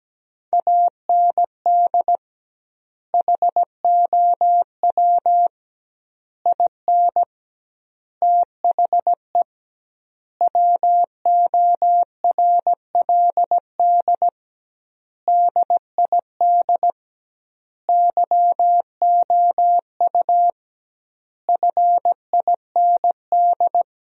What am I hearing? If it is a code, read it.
Morse code, heard as a keyed tone, about 17 wpm, 700 Hz. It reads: AND HOW IN THE WORLD DID YOU FIND
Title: 2x Word-Spacing - Sentences from Top 200 Words - 17wpm